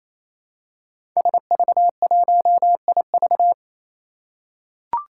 Morse code, heard as a keyed tone, about 27 wpm, 700 Hz. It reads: SV1IV
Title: Five Challenging Letter-Number Combinations 28wpm